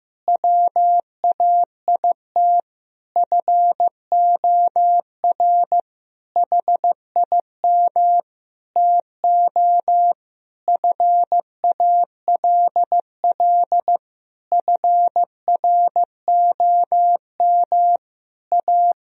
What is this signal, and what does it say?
Morse code, heard as a keyed tone, about 15 wpm, 700 Hz. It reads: WAIT FOR HIM TO FALL FROM A
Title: Sentences from Top 1000 Words - Review Part 2 - 15wpm